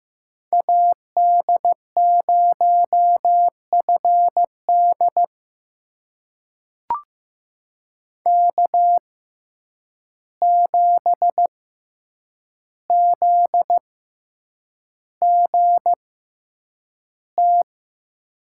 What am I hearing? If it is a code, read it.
Morse code, heard as a keyed tone, about 15 wpm, spaced as for 8 wpm, 700 Hz. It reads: AD0FD K7ZGT
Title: USA Call Signs - 5 Characters - 6x Character Spacing - 15wpm